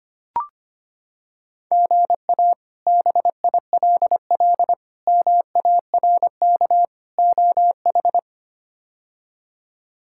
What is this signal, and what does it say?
Morse code, heard as a keyed tone, about 25 wpm, 700 Hz. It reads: GA BILL MARK OH
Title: K1USN SST QSO Practice - 25wpm